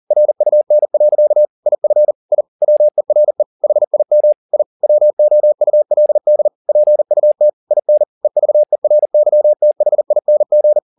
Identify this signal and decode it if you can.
Morse code, heard as a keyed tone, about 40 wpm, 600 Hz. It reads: RUN. IF I WERE HIM I WOULD PUT IN EVERYTHING